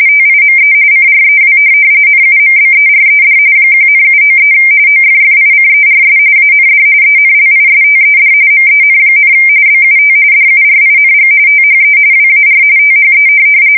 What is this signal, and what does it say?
RTTY, about 45.45 baud. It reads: TIME CREW NUMBERED 1,793 OFFICERS AND ENLISTED MEN, BUT DURING THE WAR THE